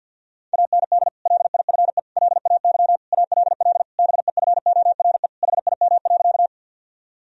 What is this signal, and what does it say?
Morse code, heard as a keyed tone, about 52 wpm, 700 Hz. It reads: AND LIFE LAY ALL BEFORE HIM.
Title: Sentences from Top 1000 Words - Review Part 2 - 50wpm